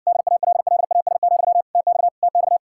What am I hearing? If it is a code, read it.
Morse code, heard as a keyed tone, about 60 wpm, 700 Hz. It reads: DALLAS, TX TX